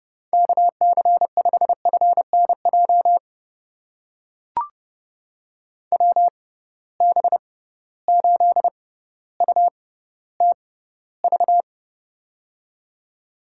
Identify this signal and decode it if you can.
Morse code, heard as a keyed tone, about 31 wpm, spaced as for 16 wpm, 700 Hz. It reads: KC5FNJ WB8UTV
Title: USA Call Signs - 6 Characters - 6x Character Spacing - 30wpm